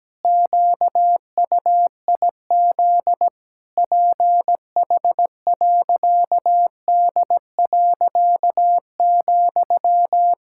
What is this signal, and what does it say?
Morse code, heard as a keyed tone, about 17 wpm, 700 Hz. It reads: QUIZ PH.D.,
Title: Pangram Challenge v2 - 17wpm